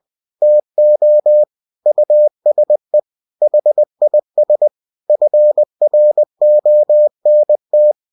Morse code, heard as a keyed tone, about 20 wpm, 600 Hz. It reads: TO USE HIS FRONT
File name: cw-books / winnie-the-pooh / winnie-20wpm-page-0017